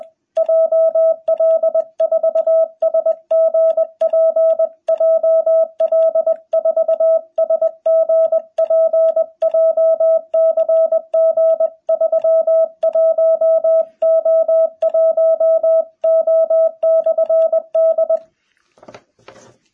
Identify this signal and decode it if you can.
Morse code, heard as a keyed tone, about 20 wpm, 660 Hz. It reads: JL4SGP JL4SGP JCG 31O1O/D
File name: jh8cbh / 2021cw